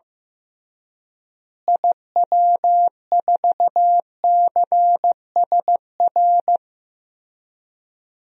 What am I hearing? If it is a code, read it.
Morse code, heard as a keyed tone, about 15 wpm, 700 Hz. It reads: IW4CSR
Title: DXCC-100 Call Signs - 6 Characters - Repeated 3x - 15wpm